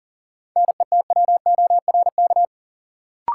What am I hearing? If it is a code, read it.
Morse code, heard as a keyed tone, about 40 wpm, 700 Hz. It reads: NETWORK
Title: Sentences from Top 1000 Words - Review Part 7 - New Word Drill - 40wpm